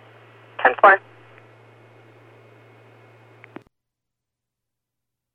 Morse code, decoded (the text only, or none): none